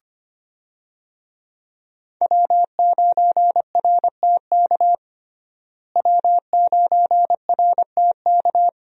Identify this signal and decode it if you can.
Morse code, heard as a keyed tone, about 25 wpm, 700 Hz. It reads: W9RTK W9RTK
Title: USA Call Signs - 5 Characters - Repeated 2x - 25wpm